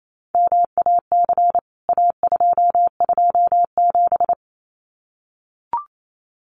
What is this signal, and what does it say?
Morse code, heard as a keyed tone, about 28 wpm, 700 Hz. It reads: MAC A227
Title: CWT Exchange 28wpm